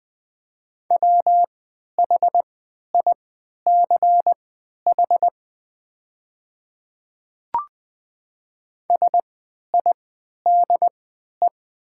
Morse code, heard as a keyed tone, about 20 wpm, spaced as for 11 wpm, 700 Hz. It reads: WHICH SIDE
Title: 3x Character Spacing - Top 100 Words - 20wpm